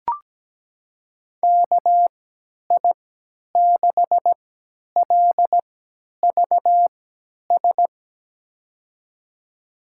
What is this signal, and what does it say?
Morse code, heard as a keyed tone, about 17 wpm, spaced as for 10 wpm, 700 Hz. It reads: KI6LVS
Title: USA Call Signs - 6 Characters - 3x Character Spacing - 17wpm